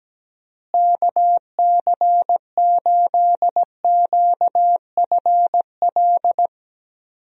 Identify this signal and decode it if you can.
Morse code, heard as a keyed tone, about 17 wpm, 700 Hz. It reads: KC8QFL
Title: USA Call Signs - 6 Characters - Repeated 3x - 17wpm